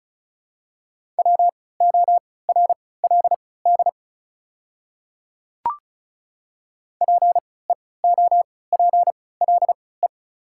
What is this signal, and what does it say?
Morse code, heard as a keyed tone, about 36 wpm, spaced as for 20 wpm, 700 Hz. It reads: WORLD PEOPLE
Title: 3x Character Spacing - Top 200 Words - 35wpm